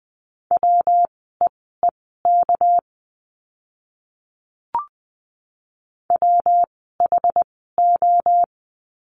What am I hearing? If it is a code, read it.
Morse code, heard as a keyed tone, about 20 wpm, spaced as for 14 wpm, 700 Hz. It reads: WEEK WHO
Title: 230 Lesson 37f - Letter x - Words 20wpm